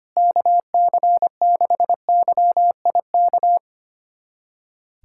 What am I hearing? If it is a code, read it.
Morse code, heard as a keyed tone, about 25 wpm, 700 Hz. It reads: KC6YIK
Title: USA Callsign DE Callsign 25wpm